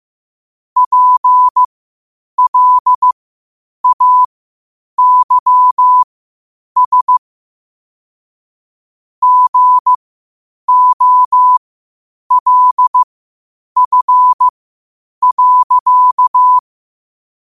Morse code, heard as a keyed tone, about 15 wpm, spaced as for 8 wpm, 1000 Hz. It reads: PLAYS GOLF.
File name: qotdfiles / 8wpm